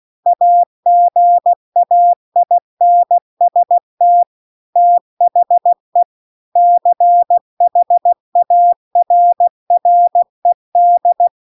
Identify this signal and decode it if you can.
Morse code, heard as a keyed tone, about 16 wpm, 700 Hz. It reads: AGAINST THE CHARRED